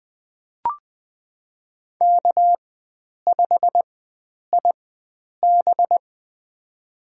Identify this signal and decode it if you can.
Morse code, heard as a keyed tone, about 20 wpm, spaced as for 9 wpm, 700 Hz. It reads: K5IB
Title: USA Call Signs - 4 Characters - 4x Character Spacing - 20wpm